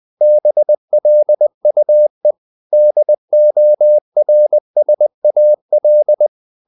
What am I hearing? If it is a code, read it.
Morse code, heard as a keyed tone, about 20 wpm, 600 Hz. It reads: BLUE DORSAL
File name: the old man and the sea part 2 20wpm